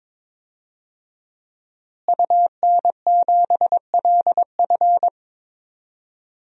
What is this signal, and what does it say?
Morse code, heard as a keyed tone, about 22 wpm, 700 Hz. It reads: UN7LF
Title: DXCC-100 Call Signs - 5 Characters - Repeated 3x - 22wpm